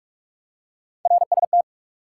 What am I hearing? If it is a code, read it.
Morse code, heard as a keyed tone, about 46 wpm, 700 Hz. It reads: RST
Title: Top 90 QSO Elements 45wpm